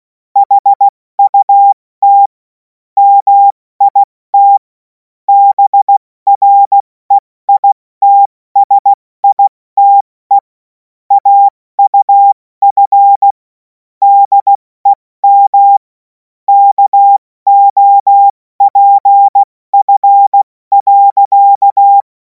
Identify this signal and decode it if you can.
Morse code, heard as a keyed tone, about 16 wpm, 800 Hz. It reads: HUT MIT BREITSITE AUF DEM KOPF.
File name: Morsekurs / Witze / Witz 26. Mai 2026 70 F80